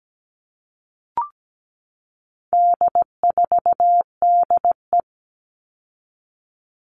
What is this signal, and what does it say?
Morse code, heard as a keyed tone, about 17 wpm, 700 Hz. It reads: D4DE
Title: All Cut Numbers - 4-Digit 17wpm